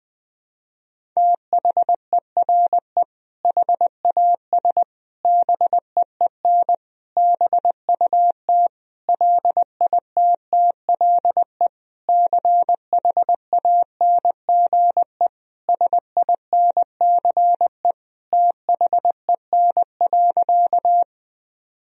Morse code, heard as a keyed tone, about 21 wpm, 700 Hz. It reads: THERE HAS BEEN BUT LITTLE CHANGE SINCE THEN.
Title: Sentences from Top 1000 Words - Review Part 1 - 20wpm